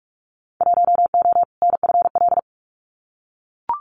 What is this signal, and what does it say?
Morse code, heard as a keyed tone, about 44 wpm, 700 Hz. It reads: 1O NFL
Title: Winter Field Day Exchanges 45wpm